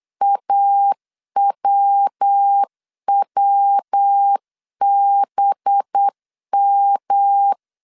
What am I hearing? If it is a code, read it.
Morse code, heard as a keyed tone, about 8 wpm, 800 Hz. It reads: AWWBM
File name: CW / DET / 1 9 1500C S A 8wpm 5637